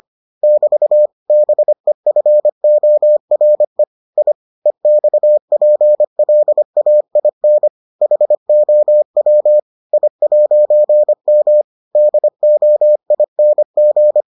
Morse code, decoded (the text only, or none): <BT> BEFORE I EXPLAIN HOW I'M DOING